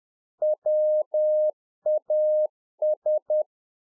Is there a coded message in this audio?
Yes: Morse code from 0.4 to 3.4 s, heard as a keyed tone, about 10 wpm, 610 Hz. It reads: WAS